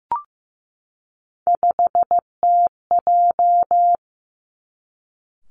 Morse code, heard as a keyed tone, about 15 wpm, 700 Hz. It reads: 5TJ
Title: Three Challenging Letter-Number Combinations 15wpm